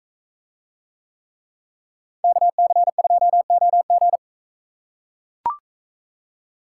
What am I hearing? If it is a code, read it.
Morse code, heard as a keyed tone, about 40 wpm, 700 Hz. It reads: KC2OG